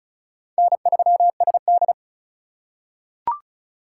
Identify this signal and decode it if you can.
Morse code, heard as a keyed tone, about 36 wpm, 700 Hz. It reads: N3SD
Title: USA Call Signs - 2x Character Spacing - 35wpm